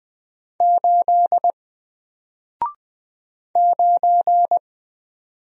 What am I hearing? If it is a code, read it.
Morse code, heard as a keyed tone, about 20 wpm, 700 Hz. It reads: 8 9
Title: Single Letter-Number Alt 20wpm